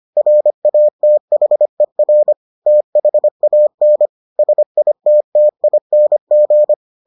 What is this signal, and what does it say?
Morse code, heard as a keyed tone, about 25 wpm, 600 Hz. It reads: RATHER THAN SITTING